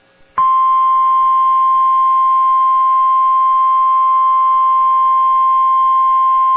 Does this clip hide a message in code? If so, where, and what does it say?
Yes: Emergency Alert System (SAME) tones (the 1050 Hz attention tone) from 0.8 to 5.6 s.